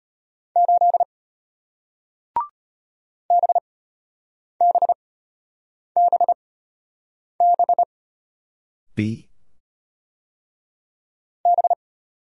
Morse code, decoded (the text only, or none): 8 B B B B B